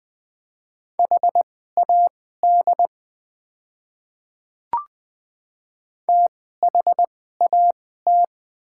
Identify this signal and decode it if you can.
Morse code, heard as a keyed tone, about 20 wpm, spaced as for 14 wpm, 700 Hz. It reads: HAD THAT
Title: 2x Character Spacing - Top 200 Words - 20wpm